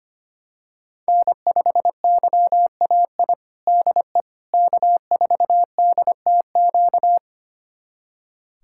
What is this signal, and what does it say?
Morse code, heard as a keyed tone, about 25 wpm, 700 Hz. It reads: N5YAI DE K4DTQ
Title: USA Callsign DE Callsign 25wpm